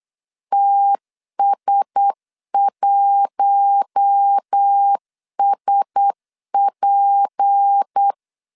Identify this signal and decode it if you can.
Morse code, heard as a keyed tone, about 8 wpm, 800 Hz. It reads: TS1SP
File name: LEZIONI / 6 PKX0 / 6 PKX0 8wpm 3341